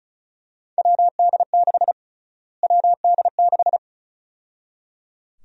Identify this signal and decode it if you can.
Morse code, heard as a keyed tone, about 36 wpm, 700 Hz. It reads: WD6 WD6
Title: USA Call Sign Prefixes - Repeated 2x - 35wpm